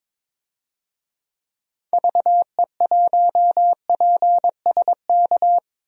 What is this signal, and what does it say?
Morse code, heard as a keyed tone, about 22 wpm, 700 Hz. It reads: VE1PSK
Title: DXCC-100 Call Signs - 6 Characters - 22wpm